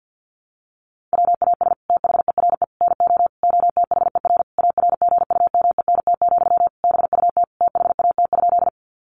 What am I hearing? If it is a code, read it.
Morse code, heard as a keyed tone, about 48 wpm, 700 Hz. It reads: WAS THERE NO OTHER ARGUMENT, BUT THAT?
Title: Sentences from Top 1400 Words 50wpm